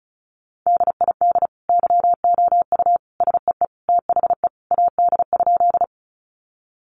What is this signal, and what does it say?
Morse code, heard as a keyed tone, about 34 wpm, 700 Hz. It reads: DID YOU SEE THE AD?
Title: Sentences from Top 1300 Words 35wpm